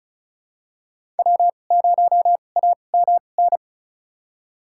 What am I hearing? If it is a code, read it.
Morse code, heard as a keyed tone, about 36 wpm, spaced as for 25 wpm, 700 Hz. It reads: W0AMN
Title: USA Call Signs - 5 Characters - 2x Character Spacing - 35wpm